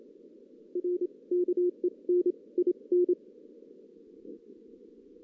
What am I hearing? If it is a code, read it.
Morse code, heard as a keyed tone, about 27 wpm, spaced as for 23 wpm, 360 Hz. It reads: R KEN IN
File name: 2016 JAN CW NAQP